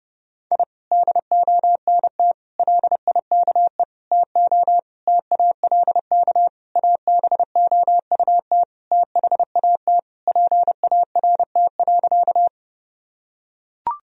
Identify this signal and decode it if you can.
Morse code, heard as a keyed tone, about 32 wpm, 700 Hz. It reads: I DONT LIKE TO TALK ABOUT THAT PART.
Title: Sentences from Top 1000 Words - Review Part 1 - 30wpm